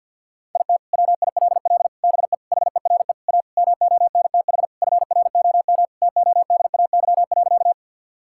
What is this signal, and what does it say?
Morse code, heard as a keyed tone, about 52 wpm, 700 Hz. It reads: IT WILL BE HERE A MONTH FROM TODAY.
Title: Sentences from Top 1000 Words - Review Part 5 - 50wpm